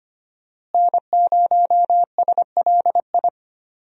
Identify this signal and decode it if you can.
Morse code, heard as a keyed tone, about 25 wpm, 700 Hz. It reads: N0SLI
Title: USA Call Signs - 5 Characters - Repeated 2x - 25wpm